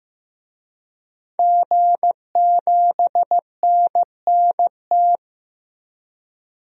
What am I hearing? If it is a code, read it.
Morse code, heard as a keyed tone, about 15 wpm, 700 Hz. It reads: G7NNT
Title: DXCC-100 Call Signs - 5 Characters - 15wpm